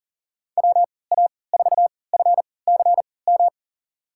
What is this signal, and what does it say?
Morse code, heard as a keyed tone, about 40 wpm, spaced as for 23 wpm, 700 Hz. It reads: WA4FCM
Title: USA Call Signs - 6 Characters - 3x Character Spacing - 40wpm